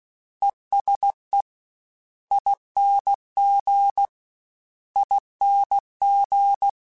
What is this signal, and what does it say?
Morse code, heard as a keyed tone, about 16 wpm, 800 Hz. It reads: ESE ING ING
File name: Special INGs-mp3s / Special INGs-mp3s / ESE ING @15